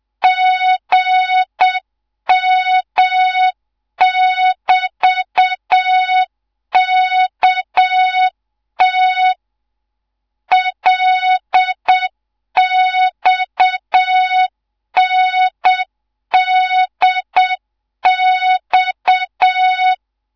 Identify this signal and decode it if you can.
Morse code, heard as a keyed tone, about 7 wpm, 3700 Hz, with noise. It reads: GM<BT>KT LXNDX